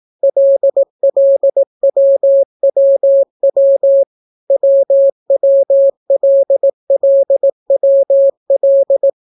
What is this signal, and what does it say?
Morse code, heard as a keyed tone, about 18 wpm, 550 Hz. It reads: LLWWW WWLLWL